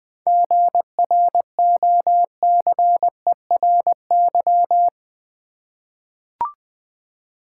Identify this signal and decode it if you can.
Morse code, heard as a keyed tone, about 21 wpm, 700 Hz. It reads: GROCERY
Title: Seven Letter Words - 20wpm